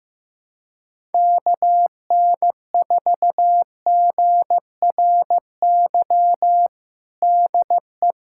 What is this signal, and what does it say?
Morse code, heard as a keyed tone, about 15 wpm, 700 Hz. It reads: KN4GRY DE
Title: USA Callsign DE Callsign 15wpm